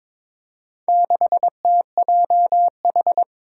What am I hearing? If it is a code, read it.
Morse code, heard as a keyed tone, about 22 wpm, 700 Hz. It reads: 6TJH
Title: Four Challenging Letter-Number Combinations 22wpm